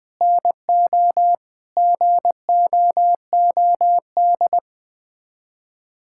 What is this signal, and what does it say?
Morse code, heard as a keyed tone, about 21 wpm, 700 Hz. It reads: NO GOOD
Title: Sets of 9 Words - 20wpm